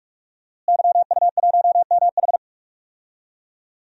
Morse code, heard as a keyed tone, about 46 wpm, 700 Hz. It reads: YU1MH
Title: DXCC-100 Call Signs - 5 Characters - Repeated 3x - 45wpm